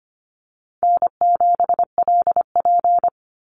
Morse code, heard as a keyed tone, about 25 wpm, 700 Hz. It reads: N7LP
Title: NA SOTA Chaser Callsigns 25wpm